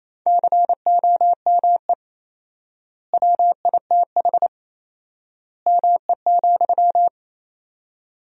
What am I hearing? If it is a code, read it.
Morse code, heard as a keyed tone, about 28 wpm, 700 Hz. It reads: COME WITH ME,
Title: 4x Word-Spacing - Sentences from Top 200 Words - 28wpm